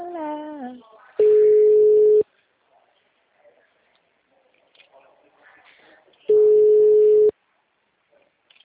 Ringback tone (Europe, 425 Hz).
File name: out- -2002- - - .10471